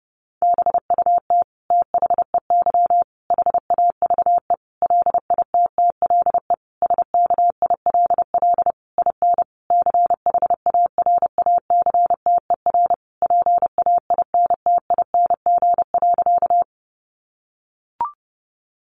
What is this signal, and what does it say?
Morse code, heard as a keyed tone, about 30 wpm, 700 Hz. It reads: BUT THEY HAVE LITTLE SKILL IN CHARACTER PAINTING.
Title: Sentences from Top 1200 Words 30wpm